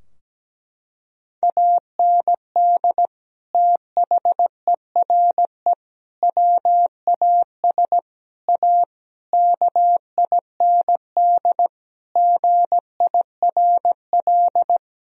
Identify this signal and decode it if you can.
Morse code, heard as a keyed tone, about 17 wpm, 700 Hz. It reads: AND THERE WAS A KIND GIRL
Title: Sentences from Top 1000 Words - Review Part 3 - 17wpm